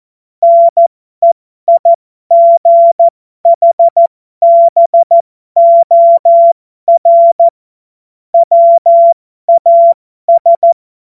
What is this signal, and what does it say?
Morse code, heard as a keyed tone, about 14 wpm, spaced as for 12 wpm, 680 Hz. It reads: NEIGHBOR WAS